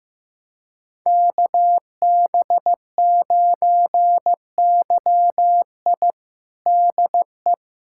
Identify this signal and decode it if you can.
Morse code, heard as a keyed tone, about 16 wpm, 700 Hz. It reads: KB9YI DE
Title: USA Callsign DE Callsign - 5 Characters - 15wpm